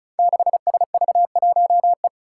Morse code, heard as a keyed tone, about 36 wpm, 700 Hz. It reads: 6SV1E
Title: Five Challenging Letter-Number Combinations 35wpm